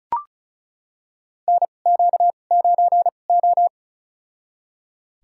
Morse code, heard as a keyed tone, about 36 wpm, spaced as for 25 wpm, 700 Hz. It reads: NQ9O
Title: USA Call Signs - 4 Characters - 2x Character Spacing - 35wpm